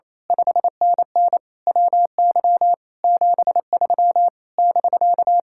Morse code, heard as a keyed tone, about 27 wpm, 700 Hz. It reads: 5NN WY 73 <BK>
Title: Speed-Racing Common QSO Phrases - 28wpm